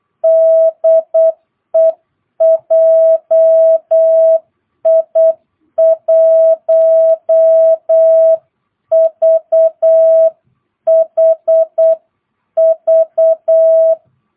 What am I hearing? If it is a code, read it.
Morse code, heard as a keyed tone, about 8 wpm, 660 Hz. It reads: DE JI1VHV
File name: jh8cbh / 2022cw